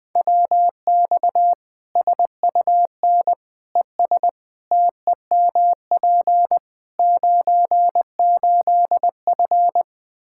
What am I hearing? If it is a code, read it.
Morse code, heard as a keyed tone, about 20 wpm, 700 Hz. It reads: WX SUN ES TEMP 98F
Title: K1USN SST QSO Practice 20wpm